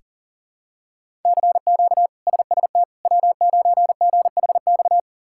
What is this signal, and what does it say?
Morse code, heard as a keyed tone, about 40 wpm, 700 Hz. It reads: CQ SST W9GHX
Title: K1USN SST QSO Practice - 40wpm